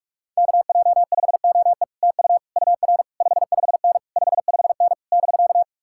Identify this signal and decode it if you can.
Morse code, heard as a keyed tone, about 49 wpm, 700 Hz. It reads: KJ5OE TU UR 55N 55N <BK>